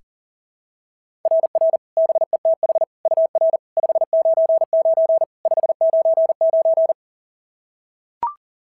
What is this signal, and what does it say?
Morse code, heard as a keyed tone, about 43 wpm, 690 Hz. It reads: RR BETH UR 599 599
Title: SOTA QSO Practice 40wpm